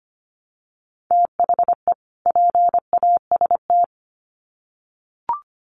Morse code, heard as a keyed tone, about 25 wpm, 700 Hz. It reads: THE PAST
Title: Sets of 2 Words 25wpm